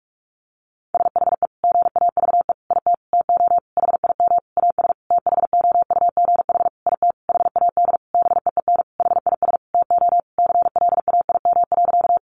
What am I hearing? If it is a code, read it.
Morse code, heard as a keyed tone, about 44 wpm, 700 Hz. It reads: SHE GAVE IT TO HIM AS THOUGH IT HAD BEEN HIS TO CLAIM.
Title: Sentences from Top 1500 Words 45wpm